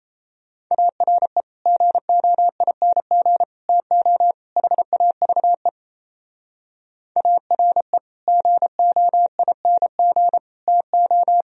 Morse code, heard as a keyed tone, about 37 wpm, 700 Hz. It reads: ARE GOING TO HAVE ARE GOING TO